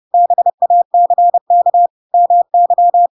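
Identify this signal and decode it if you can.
Morse code, heard as a keyed tone, about 30 wpm, 700 Hz. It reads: BACK MY
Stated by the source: Chapter 2 of The Adventures of Pinocchio read as a book in Morse code (CW) at 30 wpm, by ik1hge.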